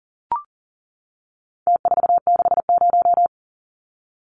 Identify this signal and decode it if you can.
Morse code, heard as a keyed tone, about 40 wpm, 700 Hz. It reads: T460
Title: Numbers - Contest Style - 4-Digit 40wpm